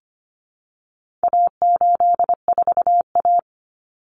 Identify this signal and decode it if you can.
Morse code, heard as a keyed tone, about 25 wpm, 700 Hz. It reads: A84A
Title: Common Cut Numbers - 4-Digit 25wpm